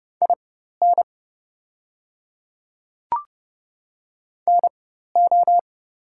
Morse code, heard as a keyed tone, about 30 wpm, spaced as for 14 wpm, 700 Hz. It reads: IN NO
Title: 4x Character Spacing - Top 100 Words - 30wpm